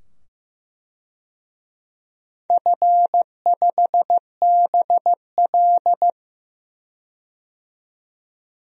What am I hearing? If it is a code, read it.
Morse code, heard as a keyed tone, about 15 wpm, 700 Hz. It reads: F5BL
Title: DXCC-100 Call Signs - 4 Characters - 15wpm